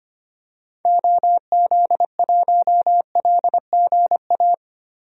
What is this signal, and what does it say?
Morse code, heard as a keyed tone, about 25 wpm, 700 Hz. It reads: OZ1LGA
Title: DXCC-100 Call Signs - 6 Characters - Repeated 2x - 25wpm